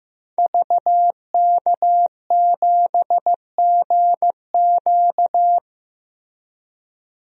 Morse code, heard as a keyed tone, about 15 wpm, 700 Hz. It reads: VK7GQ
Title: DXCC-100 Call Signs - 5 Characters - Repeated 2x - 15wpm